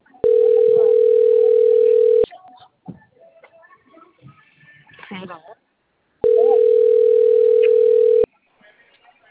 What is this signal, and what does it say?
Ringback tone (North America), 440 + 480 Hz, with speech.
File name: out- - - - - .60340